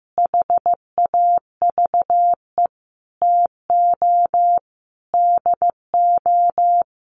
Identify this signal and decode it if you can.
Morse code, heard as a keyed tone, about 15 wpm, 700 Hz. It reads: HAVE TO DO
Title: Sets of 5 Words Encore 15wpm